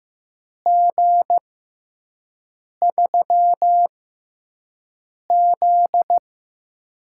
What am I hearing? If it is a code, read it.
Morse code, heard as a keyed tone, about 15 wpm, spaced as for 5 wpm, 700 Hz. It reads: G3Z
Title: USA Call Signs - 6x Character Spacing - 15wpm